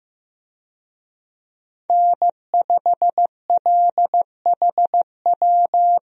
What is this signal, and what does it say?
Morse code, heard as a keyed tone, about 15 wpm, 700 Hz. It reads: N5LHW
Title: USA Call Signs - 5 Characters - Repeated 3x - 15wpm